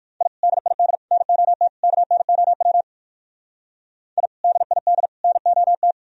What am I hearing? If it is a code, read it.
Morse code, heard as a keyed tone, about 58 wpm, 700 Hz. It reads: I DID NOT KNOW I DID NOT